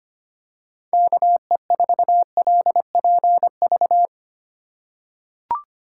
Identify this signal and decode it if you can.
Morse code, heard as a keyed tone, about 25 wpm, 700 Hz. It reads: KE4LPV
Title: USA Call Signs - 6 Characters - 2x Character Spacing - 25wpm